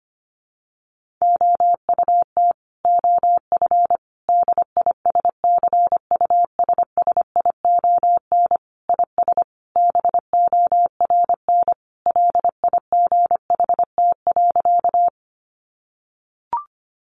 Morse code, heard as a keyed tone, about 25 wpm, 700 Hz. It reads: OUT OF DISCUSSION IS BORN LIGHT.
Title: Sentences from Top 1100 Words 25wpm